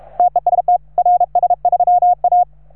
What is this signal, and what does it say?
Morse code, heard as a keyed tone, about 36 wpm, 710 Hz. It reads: TEST RS3A
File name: RS3A-20